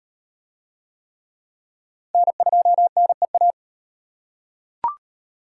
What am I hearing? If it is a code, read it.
Morse code, heard as a keyed tone, about 38 wpm, 700 Hz. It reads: N2NEA